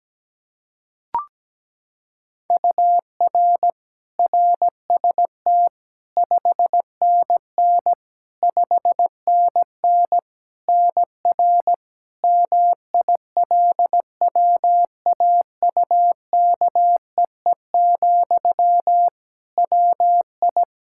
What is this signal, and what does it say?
Morse code, heard as a keyed tone, about 17 wpm, 700 Hz. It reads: UR RST 5NN 5NN NR MILWAUKEE, WI
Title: Practice QSO - Exchange 1 - Beginning - 17wpm